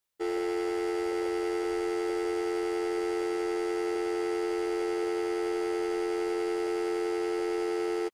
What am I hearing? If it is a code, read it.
Dial tone (North America), 350 + 440 Hz, with noise.
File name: Long beep when the phone is off-hook